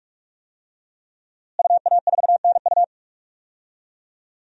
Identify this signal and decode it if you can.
Morse code, heard as a keyed tone, about 46 wpm, 700 Hz. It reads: UA4NU
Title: DXCC-100 Call Signs - 5 Characters - Repeated 3x - 45wpm